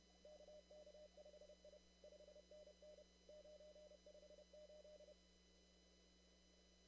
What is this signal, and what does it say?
Morse code, heard as a keyed tone, about 31 wpm, 600 Hz. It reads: KK5I 5NN 958
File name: CW / kt5j wpx cw-op-k5pi